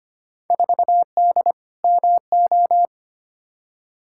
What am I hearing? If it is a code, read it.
Morse code, heard as a keyed tone, about 25 wpm, 700 Hz. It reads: 4D MO
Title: Field Day Exchanges Encore - 25wpm